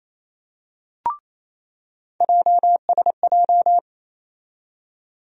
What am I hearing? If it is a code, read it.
Morse code, heard as a keyed tone, about 28 wpm, 700 Hz. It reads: JSJ
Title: USA Call Sign Postfixes - 3 Letters 28wpm